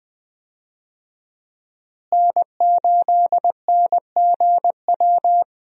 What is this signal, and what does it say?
Morse code, heard as a keyed tone, about 20 wpm, 700 Hz. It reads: N8NGW
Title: USA Call Signs - 5 Characters - Repeated 2x - 20wpm